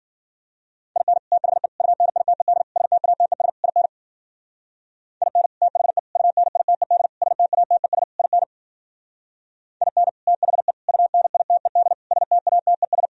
Morse code, heard as a keyed tone, about 51 wpm, 700 Hz. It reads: IN THE UNITED STATES IN IN THE UNITED STATES IN IN THE UNITED STATES